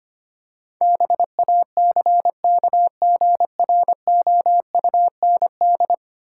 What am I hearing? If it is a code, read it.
Morse code, heard as a keyed tone, about 25 wpm, 700 Hz. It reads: BACKGROUND
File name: practice / 25wpm / Ten Letter Words - 25wpm